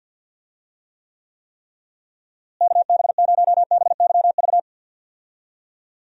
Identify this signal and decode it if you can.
Morse code, heard as a keyed tone, about 49 wpm, 700 Hz. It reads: KB0BYV
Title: USA Call Signs - 6 Characters - Repeated 3x - 50wpm